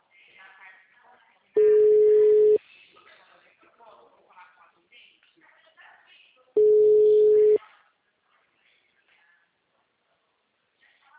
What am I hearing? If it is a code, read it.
Ringback tone (Europe, 425 Hz), with speech.